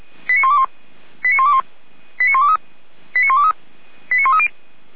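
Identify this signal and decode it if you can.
Five-tone selective call (ZVEI), heard as quick runs of five tones, tones of about 70 ms. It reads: 89121 89121 89123 89123 89130
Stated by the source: ZVEI five-tone selective call 89022 received on an SDR by feuerwehr-orlamuende, a fire brigade at Orlamünde.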